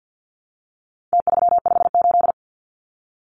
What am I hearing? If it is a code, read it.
Morse code, heard as a keyed tone, about 50 wpm, 700 Hz. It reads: T358
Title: Numbers - Contest Style - 4-Digit 50wpm